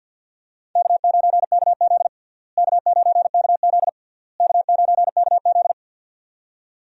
Morse code, heard as a keyed tone, about 48 wpm, 700 Hz. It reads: K9KZ K9KZ K9KZ
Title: USA Call Signs - 4 Characters - Repeated 3x - 50wpm